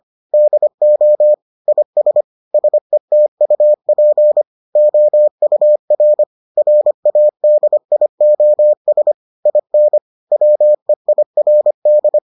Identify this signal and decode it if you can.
Morse code, heard as a keyed tone, about 25 wpm, 600 Hz. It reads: DO IS SETUP OUR RADIOS IN WEIRD